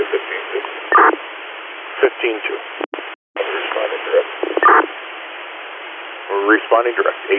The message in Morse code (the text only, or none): none